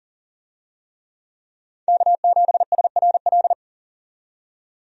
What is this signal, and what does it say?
Morse code, heard as a keyed tone, about 40 wpm, 700 Hz. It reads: K7SRL
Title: USA Call Signs - 5 Characters - 40wpm